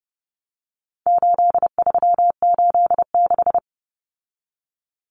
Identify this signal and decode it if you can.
Morse code, heard as a keyed tone, about 30 wpm, 700 Hz. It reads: 8386
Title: Numbers - 4-Digit 30wpm